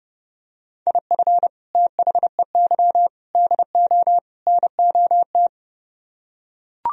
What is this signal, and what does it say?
Morse code, heard as a keyed tone, about 31 wpm, 700 Hz. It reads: IF THEY DO NOT